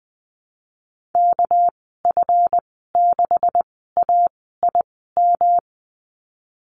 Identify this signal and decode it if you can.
Morse code, heard as a keyed tone, about 20 wpm, spaced as for 15 wpm, 700 Hz. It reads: KF6AIM
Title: 231 Lesson 37g - Letter x - Call Signs 20wpm